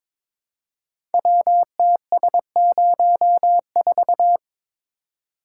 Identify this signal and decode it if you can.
Morse code, heard as a keyed tone, about 22 wpm, 700 Hz. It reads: WTS04
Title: Code Groups - 5 Letter:Numbers - 22wpm